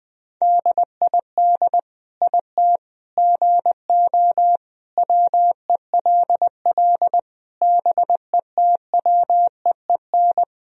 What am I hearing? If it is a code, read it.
Morse code, heard as a keyed tone, about 20 wpm, 700 Hz. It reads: DID IT GO WELL BETWEEN
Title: Sentences from Top 1000 Words - Review Part 1 - 20wpm